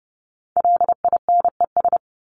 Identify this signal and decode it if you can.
Morse code, heard as a keyed tone, about 30 wpm, 700 Hz. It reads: LINES
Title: ES:S Suffix - Top 100 Words with Base Words 30wpm